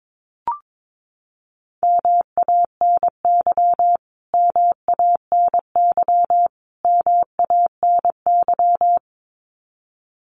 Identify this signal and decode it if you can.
Morse code, heard as a keyed tone, about 22 wpm, 700 Hz. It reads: MANY MANY MANY
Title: Top 500 Words Repeated 22wpm